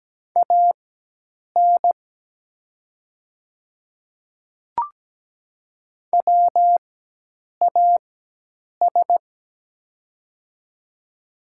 Morse code, heard as a keyed tone, about 17 wpm, spaced as for 8 wpm, 700 Hz. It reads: AN WAS
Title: 4x Character Spacing - Top 100 Words - 17wpm